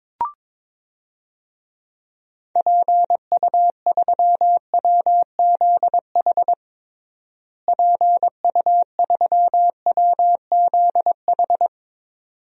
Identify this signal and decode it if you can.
Morse code, heard as a keyed tone, about 22 wpm, 700 Hz. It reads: PU3WZH PU3WZH
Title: DXCC-100 Call Signs - 6 Characters - Repeated 2x - 22wpm